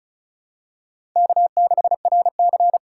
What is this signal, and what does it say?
Morse code, heard as a keyed tone, about 34 wpm, 700 Hz. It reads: K6RC
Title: K1USN SST QSO Practice - 35wpm